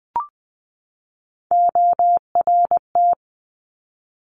Morse code, heard as a keyed tone, about 20 wpm, 700 Hz. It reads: ORT
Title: Top Three Letter Combinations in Words 20wpm